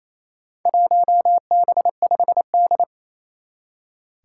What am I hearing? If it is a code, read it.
Morse code, heard as a keyed tone, about 28 wpm, 700 Hz. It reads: 1B5D
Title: Four Challenging Letter-Number Combinations 28wpm